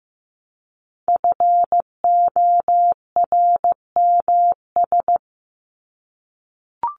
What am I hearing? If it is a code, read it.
Morse code, heard as a keyed tone, about 15 wpm, 700 Hz. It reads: FORMS
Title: ES:S Suffix - Top 100 Words with Base Words 15wpm